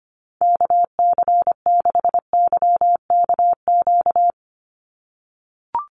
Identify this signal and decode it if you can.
Morse code, heard as a keyed tone, about 25 wpm, 700 Hz. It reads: KC6YKQ
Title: USA Call Signs 25wpm